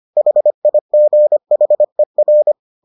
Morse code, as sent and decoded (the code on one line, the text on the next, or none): .... .. --. .... .-.
HIGHER